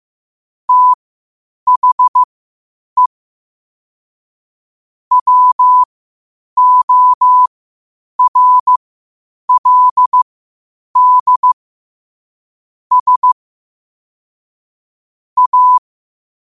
Morse code, heard as a keyed tone, about 15 wpm, spaced as for 8 wpm, 1000 Hz. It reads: THE WORLD S A
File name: 8wpm